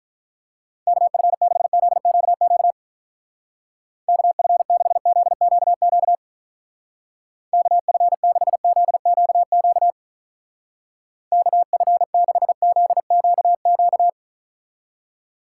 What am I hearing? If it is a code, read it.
Morse code, heard as a keyed tone, about 44 wpm, 700 Hz. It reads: KF6ZQQ KF6ZQQ KF6ZQQ KF6ZQQ